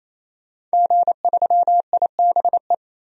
Morse code, heard as a keyed tone, about 28 wpm, 700 Hz. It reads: G3IBE
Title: DXCC-100 Call Signs - 5 Characters - 28wpm